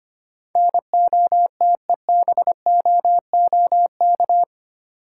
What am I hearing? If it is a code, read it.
Morse code, heard as a keyed tone, about 25 wpm, 700 Hz. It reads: NOTEBOOK
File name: 8 Letter Words - First 3 Letters Given - 25wpm